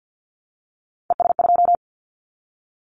Morse code, heard as a keyed tone, about 50 wpm, 700 Hz. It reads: ES2
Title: Callsign Prefixes for Top 100 DXCC - 50wpm